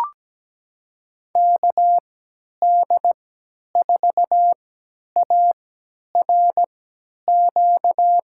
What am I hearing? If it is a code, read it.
Morse code, heard as a keyed tone, about 17 wpm, spaced as for 10 wpm, 700 Hz. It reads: KD4ARQ
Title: USA Call Signs - 6 Characters - 3x Character Spacing - 17wpm